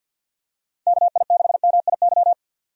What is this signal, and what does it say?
Morse code, heard as a keyed tone, about 51 wpm, 700 Hz. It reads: KI6MIY
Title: USA Call Signs - 6 Characters - Repeated 2x - 50wpm